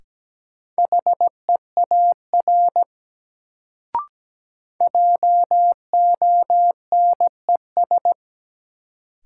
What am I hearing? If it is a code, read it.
Morse code, heard as a keyed tone, about 17 wpm, 700 Hz. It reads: HEAR JONES
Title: Sets of 5 Words - New Word Drill - 17wpm